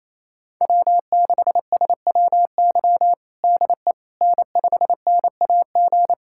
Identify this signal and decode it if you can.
Morse code, heard as a keyed tone, about 28 wpm, 700 Hz. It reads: W6SWY DE N5NAG
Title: USA Callsign DE Callsign 28wpm